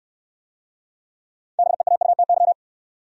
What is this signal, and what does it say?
Morse code, heard as a keyed tone, about 68 wpm, 700 Hz. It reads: BEAUTY